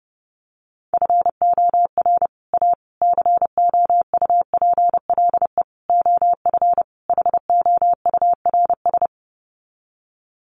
Morse code, heard as a keyed tone, about 30 wpm, 700 Hz. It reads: FOR A COUPLE OF HOURS
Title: Sets of 5 Words Encore2 30wpm